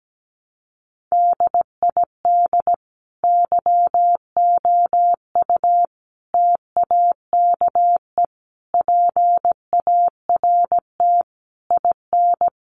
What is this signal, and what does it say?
Morse code, heard as a keyed tone, about 17 wpm, 700 Hz. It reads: DID YOU TAKE PART IN
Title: Sentences from Top 1200 Words 17wpm